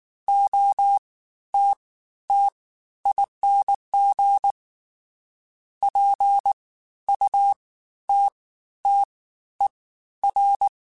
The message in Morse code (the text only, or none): OTTING PUTTER